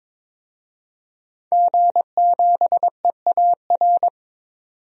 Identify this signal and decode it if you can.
Morse code, heard as a keyed tone, about 22 wpm, 700 Hz. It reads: G7EAR
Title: DXCC-100 Call Signs - 5 Characters - 22wpm